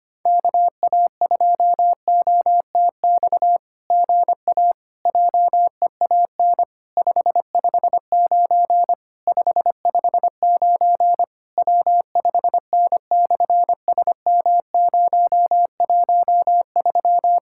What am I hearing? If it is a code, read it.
Morse code, heard as a keyed tone, about 24 wpm, 700 Hz. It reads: KA2OTX GA JEAN 559 559 W5N/SM013
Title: SOTA QSO Practice 25wpm